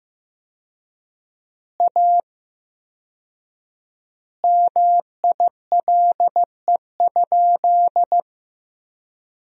Morse code, heard as a keyed tone, about 15 wpm, 700 Hz. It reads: A MILE?
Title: 4x Word-Spacing - Sentences from Top 400 Words - 15wpm